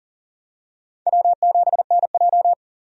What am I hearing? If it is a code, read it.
Morse code, heard as a keyed tone, about 37 wpm, 700 Hz. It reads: W7NJ
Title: USA Call Signs - 4 Characters - 40wpm